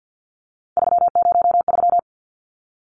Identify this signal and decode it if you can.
Morse code, heard as a keyed tone, about 50 wpm, 700 Hz. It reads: 303
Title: Numbers - 3-Digit 50wpm